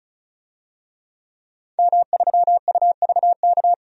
Morse code, heard as a keyed tone, about 34 wpm, 700 Hz. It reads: M3UVK
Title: DXCC-100 Call Signs - 5 Characters - 35wpm